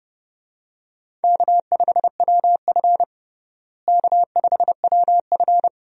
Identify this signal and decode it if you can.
Morse code, heard as a keyed tone, about 30 wpm, 700 Hz. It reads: K5WF K5WF
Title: USA Call Signs - 4 Characters - Repeated 3x - 30wpm